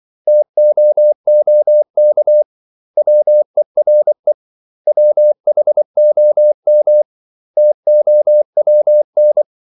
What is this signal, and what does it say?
Morse code, heard as a keyed tone, about 24 wpm, 600 Hz. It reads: TOOK WERE WHOM TOWN
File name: CW / Practice / 500MostCommonEnglishWords24